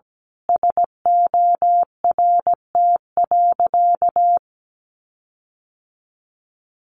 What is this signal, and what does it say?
Morse code, heard as a keyed tone, about 17 wpm, 700 Hz. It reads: SORT.
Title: Sentences from Top 1500 Words 17wpm